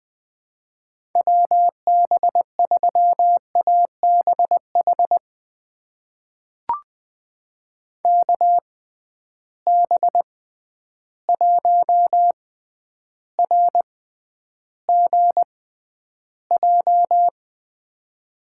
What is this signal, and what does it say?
Morse code, heard as a keyed tone, about 20 wpm, spaced as for 10 wpm, 700 Hz. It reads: WB3ABH KB1RGJ
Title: USA Call Signs - 6 Characters - 6x Character Spacing - 20wpm